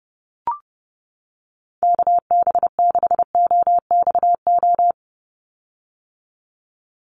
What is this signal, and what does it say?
Morse code, heard as a keyed tone, about 30 wpm, 700 Hz. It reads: KB6OXO
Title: USA Call Signs 30wpm